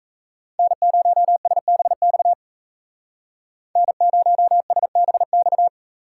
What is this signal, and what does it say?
Morse code, heard as a keyed tone, about 36 wpm, 700 Hz. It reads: N0SBX N0SBX